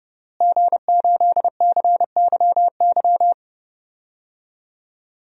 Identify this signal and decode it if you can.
Morse code, heard as a keyed tone, about 30 wpm, 700 Hz. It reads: G8CYY
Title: DXCC-100 Call Signs - 5 Characters - Repeated 3x - 30wpm